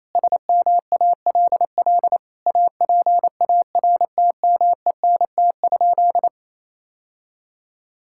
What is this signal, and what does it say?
Morse code, heard as a keyed tone, about 27 wpm, 700 Hz. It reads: SMALL APARTMENT?
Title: Sentences from Top 1000 Words - Review Part 6 - 28wpm